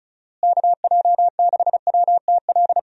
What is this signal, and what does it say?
Morse code, heard as a keyed tone, about 36 wpm, 700 Hz. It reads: KJ6WTL
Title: USA Call Signs - 6 Characters - Repeated 3x - 35wpm